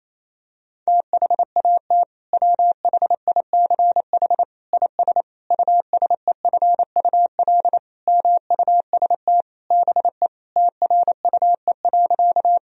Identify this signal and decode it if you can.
Morse code, heard as a keyed tone, about 28 wpm, 700 Hz. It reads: THAT WHICH IS USEFUL MUST BE TRUE.
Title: Sentences from Top 1000 Words - Review Part 4 - 28wpm